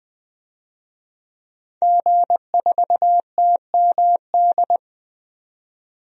Morse code, heard as a keyed tone, about 20 wpm, 700 Hz. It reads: G4TMD
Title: DXCC-100 Call Signs - 5 Characters - 20wpm